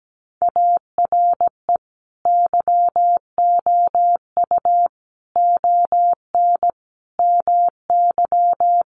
Morse code, heard as a keyed tone, about 17 wpm, 700 Hz. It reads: ARE YOU ON MY
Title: Sentences from Top 1500 Words 17wpm